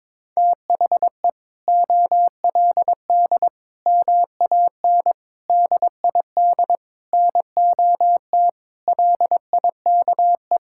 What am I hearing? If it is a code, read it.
Morse code, heard as a keyed tone, about 22 wpm, 700 Hz. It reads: THE OLD MAN DID NOT LIKE
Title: Sentences from Top 1000 Words - Review Part 6 - 22wpm